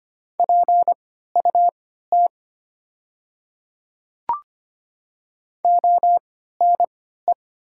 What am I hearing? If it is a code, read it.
Morse code, heard as a keyed tone, about 25 wpm, spaced as for 14 wpm, 700 Hz. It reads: PUT ONE
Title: 3x Character Spacing - Top 200 Words - 25wpm